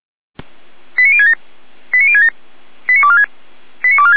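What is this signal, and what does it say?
Five-tone selective call (ZVEI), heard as quick runs of five tones, tones of about 70 ms. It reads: 89067 89067 89246 89246
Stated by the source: ZVEI five-tone selective call received on SDR, 455 channel path.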